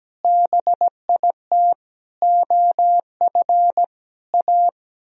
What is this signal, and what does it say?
Morse code, heard as a keyed tone, about 17 wpm, 700 Hz. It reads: BIT OF A
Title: Speed-Racing - Sets of 5 Words 17wpm